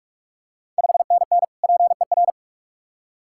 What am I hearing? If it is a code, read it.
Morse code, heard as a keyed tone, about 46 wpm, 700 Hz. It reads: 5NN PER
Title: Ontario QSO Party Exchange 45wpm